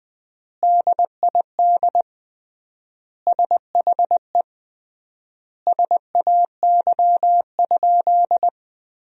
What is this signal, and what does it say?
Morse code, heard as a keyed tone, about 20 wpm, 700 Hz. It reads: DID SHE SAY?
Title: 3x Word-Spacing - Sentences from Top 200 Words - 20wpm